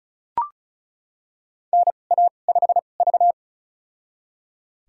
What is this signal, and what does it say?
Morse code, heard as a keyed tone, about 34 wpm, spaced as for 25 wpm, 700 Hz. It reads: NA5V
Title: USA Call Signs - 4 Characters - 2x Character Spacing - 35wpm